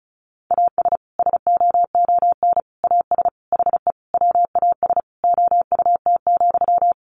Morse code, heard as a keyed tone, about 34 wpm, 700 Hz. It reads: AS SOON AS HE WAS OUT,
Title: Sentences from Top 1500 Words 35wpm